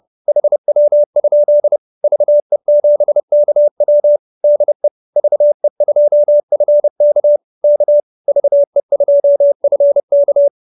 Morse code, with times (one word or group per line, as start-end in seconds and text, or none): 0.3-1.8 HW?
2.0-4.2 VE7KW
4.4-4.9 DE
5.2-7.4 VE2FK
7.6-8.0 K
8.3-10.5 VE2FK